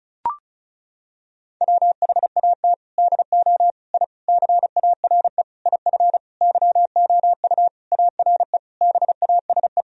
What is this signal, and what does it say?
Morse code, heard as a keyed tone, about 34 wpm, 700 Hz. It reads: WHAT DO I CARE IF YOU ARE BASE
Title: Sentences from Top 1000 Words - Review Part 1 - 35wpm